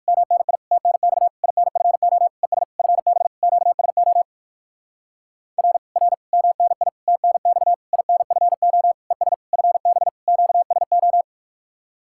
Morse code, heard as a keyed tone, about 60 wpm, 700 Hz. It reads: MNI TNX INFO ES FB QSO R R MNI TNX INFO ES FB QSO